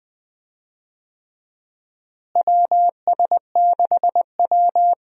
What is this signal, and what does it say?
Morse code, heard as a keyed tone, about 20 wpm, 700 Hz. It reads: WS6W
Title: USA Call Signs - 4 Characters - Repeated 3x - 20wpm